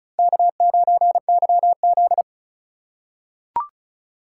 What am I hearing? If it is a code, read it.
Morse code, heard as a keyed tone, about 36 wpm, 700 Hz. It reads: K9YZ
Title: USA Call Signs - 4 Characters - 2x Character Spacing - 35wpm